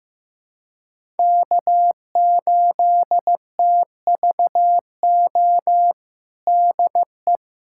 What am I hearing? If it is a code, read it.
Morse code, heard as a keyed tone, about 15 wpm, 700 Hz. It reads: K8TVO DE
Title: USA Callsign DE Callsign - 5 Characters - 15wpm